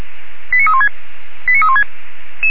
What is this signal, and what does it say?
Five-tone selective call (ZVEI), heard as quick runs of five tones, tones of about 70 ms. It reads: 89316 89316